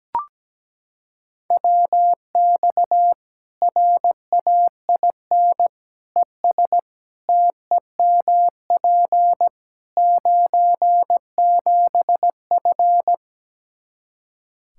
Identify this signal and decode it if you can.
Morse code, heard as a keyed tone, about 17 wpm, 700 Hz. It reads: WX RAIN ES TEMP 97F
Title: Practice QSO - Exchange 2 - Info - 17wpm